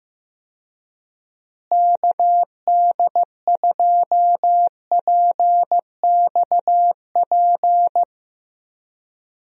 Morse code, heard as a keyed tone, about 15 wpm, 700 Hz. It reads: KD2PXP
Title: USA Call Signs - 6 Characters - Repeated 3x - 15wpm